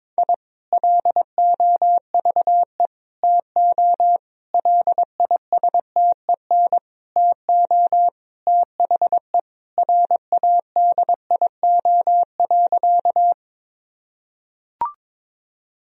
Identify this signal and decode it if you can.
Morse code, heard as a keyed tone, about 22 wpm, 700 Hz. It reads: I LOVE TO LISTEN TO THE RADIO.
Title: Sentences from Top 1000 Words - Review Part 3 - 22wpm